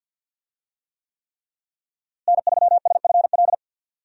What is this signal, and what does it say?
Morse code, heard as a keyed tone, about 52 wpm, 700 Hz. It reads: N3SFL
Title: USA Call Signs - 5 Characters - Repeated 3x - 50wpm